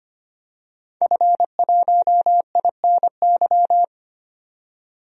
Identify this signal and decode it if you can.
Morse code, heard as a keyed tone, about 25 wpm, 700 Hz. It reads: F1INY
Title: DXCC-100 Call Signs - 5 Characters - 25wpm